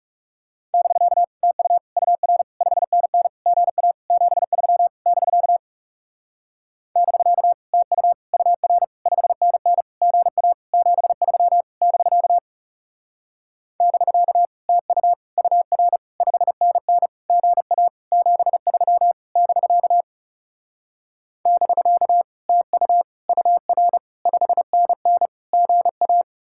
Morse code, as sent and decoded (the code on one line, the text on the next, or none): -...-.- / - ..- / ..- .-. / ..... -. -. / --. .- / --... ...-- / -...-.- / -...-.- / - ..- / ..- .-. / ..... -. -. / --. .- / --... ...-- / -...-.- / -...-.- / - ..- / ..- .-. / ..... -. -. / --. .- / --... ...-- / -...-.- / -...-.- / - ..- / ..- .-. / ..... -. -. / --. .-
<BK> TU UR 5NN GA 73 <BK> <BK> TU UR 5NN GA 73 <BK> <BK> TU UR 5NN GA 73 <BK> <BK> TU UR 5NN GA